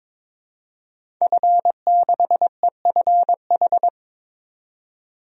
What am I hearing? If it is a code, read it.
Morse code, heard as a keyed tone, about 22 wpm, 700 Hz. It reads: F6EFH
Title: DXCC-100 Call Signs - 5 Characters - 22wpm